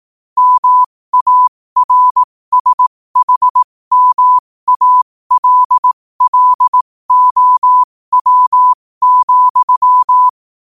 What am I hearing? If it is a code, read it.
Morse code, heard as a keyed tone, about 18 wpm, spaced as for 15 wpm, 1000 Hz. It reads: MARSHMALLOW,